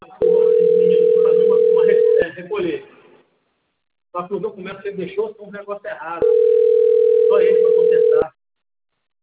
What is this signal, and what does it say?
Ringback tone (North America), 440 + 480 Hz, with speech.
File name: out- - - - - .75343